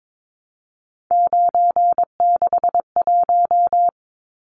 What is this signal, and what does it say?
Morse code, heard as a keyed tone, about 22 wpm, 700 Hz. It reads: 961
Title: Numbers - 3-Digit 22wpm